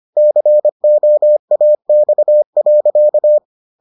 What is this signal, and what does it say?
Morse code, heard as a keyed tone, about 25 wpm, 600 Hz. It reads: COAX.